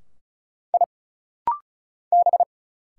Morse code, heard as a keyed tone, about 36 wpm, 700 Hz. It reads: I B
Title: Single Letters - Rapid-Fire 35wpm